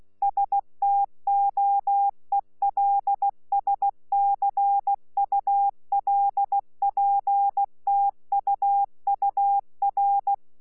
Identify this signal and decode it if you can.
Morse code, heard as a keyed tone, about 16 wpm, 800 Hz. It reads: STOELSCULPTUUR
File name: Z PI4VRZ / PI4VRZ 14-jun-2025